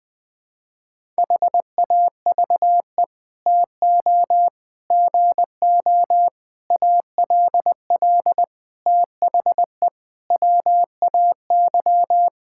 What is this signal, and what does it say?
Morse code, heard as a keyed tone, about 20 wpm, 700 Hz. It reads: HAVE TO GO ALL THE WAY
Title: Sets of 8 Words - 20wpm